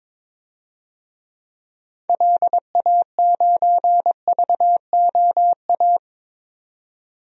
Morse code, heard as a keyed tone, about 22 wpm, 700 Hz. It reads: LA9VOA
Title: DXCC-100 Call Signs - 6 Characters - 22wpm